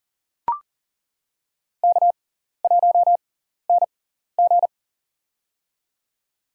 Morse code, heard as a keyed tone, about 40 wpm, spaced as for 14 wpm, 700 Hz. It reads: K1NG
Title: USA Call Signs - 4 Characters - 6x Character Spacing - 40wpm